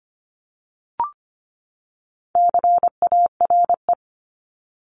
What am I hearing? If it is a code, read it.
Morse code, heard as a keyed tone, about 25 wpm, 700 Hz. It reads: CARE
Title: Top 500 Words 25wpm